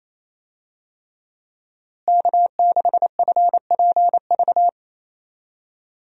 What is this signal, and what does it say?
Morse code, heard as a keyed tone, about 28 wpm, 700 Hz. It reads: K6FPV
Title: USA Call Signs - 5 Characters - Repeated 3x - 28wpm